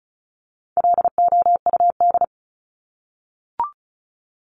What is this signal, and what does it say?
Morse code, heard as a keyed tone, about 34 wpm, 700 Hz. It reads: LOUD
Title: Top 1751-2000 Words 35wpm